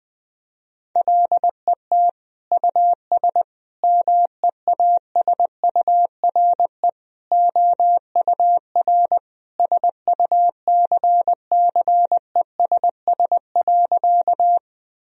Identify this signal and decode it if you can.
Morse code, heard as a keyed tone, about 21 wpm, 700 Hz. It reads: LET US MEASURE OUR SUCCESS.
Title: Sentences from Top 1000 Words - Review Part 2 - 20wpm